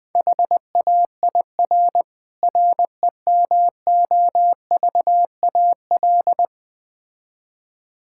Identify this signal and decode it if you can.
Morse code, heard as a keyed tone, about 20 wpm, 700 Hz. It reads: HAIR REMOVAL
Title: Difficult Words - Speed-Racing with Hints - 20wpm